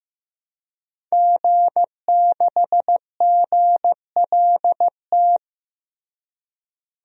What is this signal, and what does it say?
Morse code, heard as a keyed tone, about 15 wpm, 700 Hz. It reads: G6GLT
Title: DXCC-100 Call Signs - 5 Characters - Repeated 2x - 15wpm